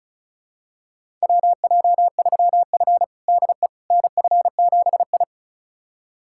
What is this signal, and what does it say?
Morse code, heard as a keyed tone, about 36 wpm, 700 Hz. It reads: WJ3F DE NF7I
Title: USA Callsign DE Callsign - 4 Characters - 35wpm